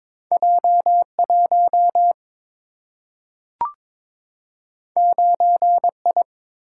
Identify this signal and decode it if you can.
Morse code, heard as a keyed tone, about 23 wpm, 700 Hz. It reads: J1 9I
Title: Single-Letter Single-Number 22wpm